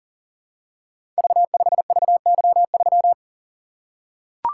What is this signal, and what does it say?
Morse code, heard as a keyed tone, about 40 wpm, 700 Hz. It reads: V5VY3
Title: Code Groups - 5 Letter:Numbers - 40wpm